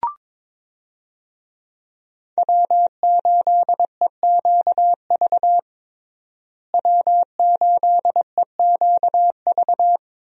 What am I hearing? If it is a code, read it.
Morse code, heard as a keyed tone, about 22 wpm, 700 Hz. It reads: W8EQV W8EQV
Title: USA Call Signs - 5 Characters - Repeated 3x - 22wpm